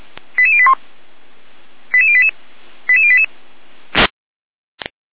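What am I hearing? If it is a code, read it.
Five-tone selective call (ZVEI), heard as quick runs of five tones, tones of about 70 ms. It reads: 88081 88088 88088
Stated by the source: ZVEI five-tone selective call received on SDR.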